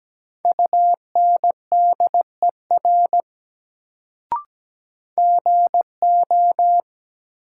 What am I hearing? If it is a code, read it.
Morse code, heard as a keyed tone, about 17 wpm, 700 Hz. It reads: UNDER GO
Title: Sets of 2 Words - New Word Drill - 17wpm